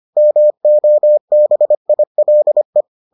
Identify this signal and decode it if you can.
Morse code, heard as a keyed tone, about 25 wpm, 600 Hz. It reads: MOBILE